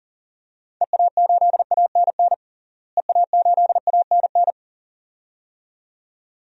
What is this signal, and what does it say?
Morse code, heard as a keyed tone, about 37 wpm, 700 Hz. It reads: EA8ANN EA8ANN
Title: DXCC-100 Call Signs - 6 Characters - Repeated 2x - 40wpm